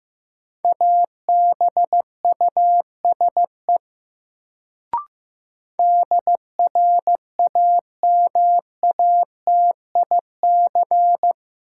Morse code, heard as a keyed tone, about 15 wpm, 700 Hz. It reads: ABUSE DRAMATIC
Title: Sentences from Top 1300 Words - New Word Drill - 15wpm